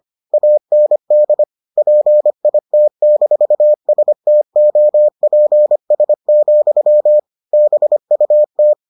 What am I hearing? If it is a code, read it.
Morse code, heard as a keyed tone, about 25 wpm, 600 Hz. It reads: AND PIT-STOPS, BUT